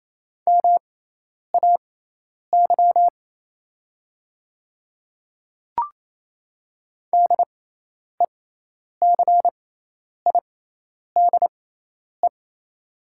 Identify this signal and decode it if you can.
Morse code, heard as a keyed tone, about 28 wpm, spaced as for 10 wpm, 690 Hz. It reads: MAY DECIDE
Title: 6x Character Spacing - Top 500 Words - 28wpm